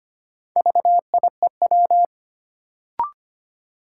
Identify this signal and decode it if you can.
Morse code, heard as a keyed tone, about 25 wpm, 700 Hz. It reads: VIEW
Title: Sets of 5 Words - New Word Drill - 25wpm